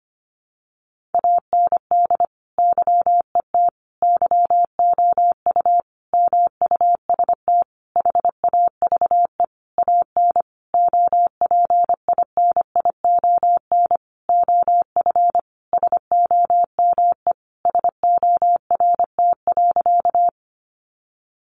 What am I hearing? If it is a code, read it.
Morse code, heard as a keyed tone, about 25 wpm, 700 Hz. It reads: AND YET YOU MUST HAVE AN OPINION OF SOME SORT.
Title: Sentences from Top 1300 Words 25wpm